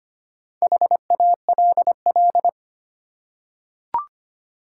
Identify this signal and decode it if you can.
Morse code, heard as a keyed tone, about 25 wpm, 700 Hz. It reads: HALL
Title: Four Letter Words - 25wpm